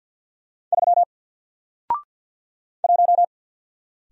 Morse code, heard as a keyed tone, about 48 wpm, 700 Hz. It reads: 3 1
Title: Single Character (Letter:Number:Punctuation:Prosign) Alt 50wpm